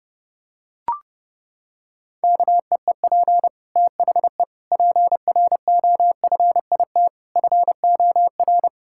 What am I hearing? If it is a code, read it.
Morse code, heard as a keyed tone, about 32 wpm, 700 Hz. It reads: KEEP THE PROFIT FOR
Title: Sentences from Top 1000 Words - Review Part 5 - 30wpm